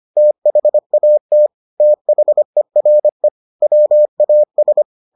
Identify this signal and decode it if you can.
Morse code, heard as a keyed tone, about 25 wpm, 600 Hz. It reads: THAT THERE WAS